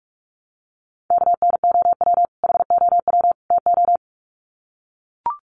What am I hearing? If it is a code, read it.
Morse code, heard as a keyed tone, about 44 wpm, 700 Hz. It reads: KNOW HOW TO
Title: Sets of 3 Words 45wpm